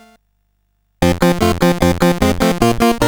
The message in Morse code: none